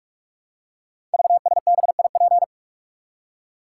Morse code, heard as a keyed tone, about 46 wpm, 700 Hz. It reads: VSBIP
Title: Code Groups - 5 Letter:Numbers - 45wpm